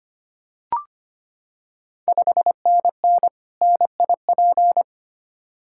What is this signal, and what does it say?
Morse code, heard as a keyed tone, about 25 wpm, 700 Hz. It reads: 5NN NIP
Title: Ontario QSO Party Exchange 25wpm